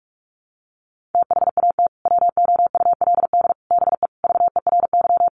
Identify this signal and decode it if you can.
Morse code, heard as a keyed tone, about 44 wpm, 700 Hz. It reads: THAT WOULD BE VERY
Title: Sentences from Top 200 Words 45wpm